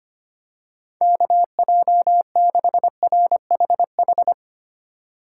Morse code, heard as a keyed tone, about 25 wpm, 700 Hz. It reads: KJ6RHH
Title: USA Call Signs - 6 Characters - 2x Character Spacing - 25wpm